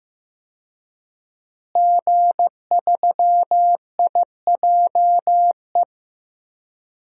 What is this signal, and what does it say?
Morse code, heard as a keyed tone, about 16 wpm, 700 Hz. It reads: G3IJE
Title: DXCC-100 Call Signs - 5 Characters - Repeated 2x - 15wpm